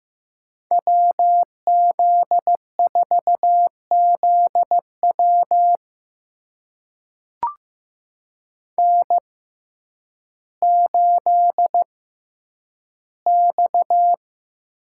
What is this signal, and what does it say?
Morse code, heard as a keyed tone, about 15 wpm, spaced as for 9 wpm, 700 Hz. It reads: WZ4ZW N8X
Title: USA Call Signs - 6x Character Spacing - 15wpm